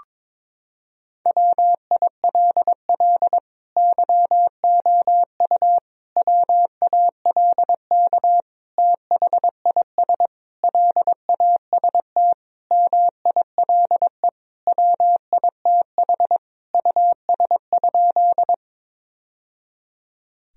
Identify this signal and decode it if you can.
Morse code, heard as a keyed tone, about 22 wpm, 700 Hz. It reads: WILL YOU WALK THIS LAST MILE WITH US?
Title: Sentences from Top 1000 Words - Review Part 1 - 22wpm